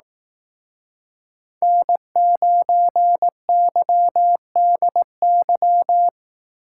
Morse code, heard as a keyed tone, about 18 wpm, 700 Hz. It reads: N9YDY
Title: Speed Racing - USA 1x3 Call Signs 15wpm